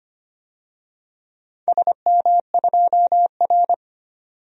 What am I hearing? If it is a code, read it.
Morse code, heard as a keyed tone, about 25 wpm, 700 Hz. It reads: SM2R
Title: DXCC-100 Call Signs - 4 Characters - 25wpm